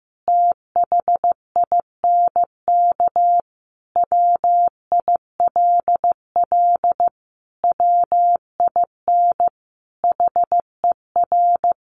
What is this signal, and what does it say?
Morse code, heard as a keyed tone, about 14 wpm, 700 Hz. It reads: THINK WILL WIN HER
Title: Sentences from Top 1500 Words 15wpm